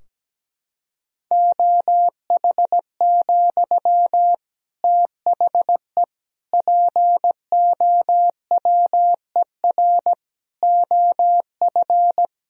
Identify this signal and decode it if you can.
Morse code, heard as a keyed tone, about 17 wpm, 700 Hz. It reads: OH, THE POWER OF
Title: Sentences from Top 1000 Words - Review Part 3 - 17wpm